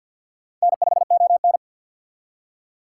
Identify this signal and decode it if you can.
Morse code, heard as a keyed tone, about 59 wpm, 700 Hz. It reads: N5ON
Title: Speed Racing - USA Call Signs 50wpm